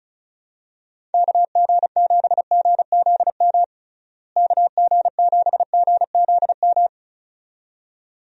Morse code, heard as a keyed tone, about 34 wpm, 700 Hz. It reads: KG7GZM KG7GZM
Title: USA Call Signs - 6 Characters - Repeated 2x - 35wpm